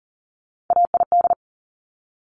Morse code, heard as a keyed tone, about 39 wpm, 700 Hz. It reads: AID
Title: Top Three Letter Combinations in Words 40wpm